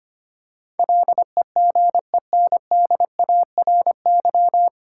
Morse code, heard as a keyed tone, about 25 wpm, 700 Hz. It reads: LEGENDARY
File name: practice / 25wpm / 9 Letter Words - First 3 Letters Given - 25wpm